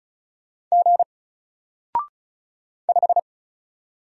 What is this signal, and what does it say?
Morse code, heard as a keyed tone, about 34 wpm, 700 Hz. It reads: G 5
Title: Single Letter-Number 35wpm